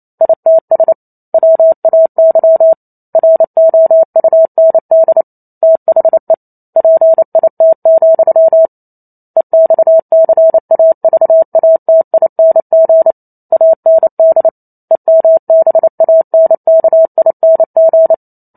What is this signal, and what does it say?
Morse code, heard as a keyed tone, about 28 wpm, 650 Hz. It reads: ITS WAY ROUND THE PIT, EXCAVATING AND EMBANKING